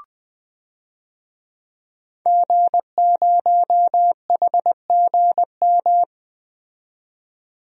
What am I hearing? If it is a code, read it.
Morse code, heard as a keyed tone, about 20 wpm, 700 Hz. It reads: G0HGM
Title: DXCC-100 Call Signs - 5 Characters - 20wpm